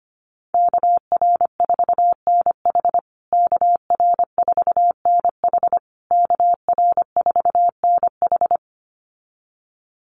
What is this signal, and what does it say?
Morse code, heard as a keyed tone, about 25 wpm, 700 Hz. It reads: KR4NH KR4NH KR4NH
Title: USA Call Signs Repeated 3x 25wpm